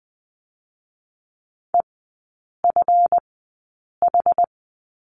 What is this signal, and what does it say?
Morse code, heard as a keyed tone, about 20 wpm, spaced as for 8 wpm, 700 Hz. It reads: EFH
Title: 159 Lesson 26e - Letter g - Three Characters 20wpm